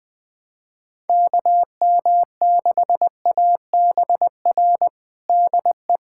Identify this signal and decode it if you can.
Morse code, heard as a keyed tone, about 20 wpm, 700 Hz. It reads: KM6ABR DE
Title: USA Callsign DE Callsign 20wpm